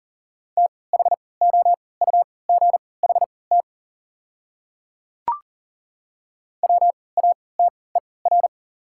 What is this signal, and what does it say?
Morse code, heard as a keyed tone, about 37 wpm, spaced as for 22 wpm, 700 Hz. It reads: THOUGHT WATER
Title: 3x Character Spacing - Top 300 Words - 40wpm